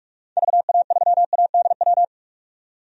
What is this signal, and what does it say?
Morse code, heard as a keyed tone, about 46 wpm, 700 Hz. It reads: VA3ADW
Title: DXCC-100 Call Signs - 6 Characters - 45wpm